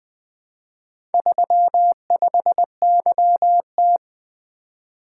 Morse code, heard as a keyed tone, about 20 wpm, 700 Hz. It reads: 35YT
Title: Four Letter-Number Combinations - 20wpm